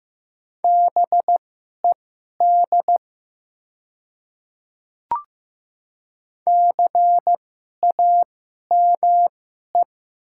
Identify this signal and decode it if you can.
Morse code, heard as a keyed tone, about 15 wpm, spaced as for 11 wpm, 690 Hz. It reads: BED CAME
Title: 2x Character Spacing - Top 500 Words - 15wpm